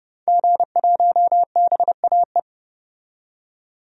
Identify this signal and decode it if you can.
Morse code, heard as a keyed tone, about 30 wpm, 700 Hz. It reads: G1BAE
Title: DXCC-100 Call Signs - 5 Characters - 30wpm